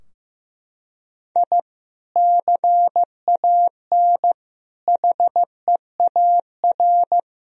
Morse code, heard as a keyed tone, about 16 wpm, 700 Hz. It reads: I CAN HEAR
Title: Sentences from Top 1000 Words - Review Part 4 - 15wpm